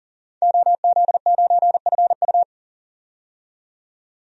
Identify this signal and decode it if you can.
Morse code, heard as a keyed tone, about 40 wpm, 700 Hz. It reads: OZ9FU
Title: DXCC-100 Call Signs - 5 Characters - 40wpm